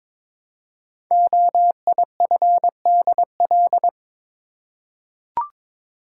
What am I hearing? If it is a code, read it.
Morse code, heard as a keyed tone, about 22 wpm, 700 Hz. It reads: OIFDL
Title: Code Groups - 5 Letters - 22wpm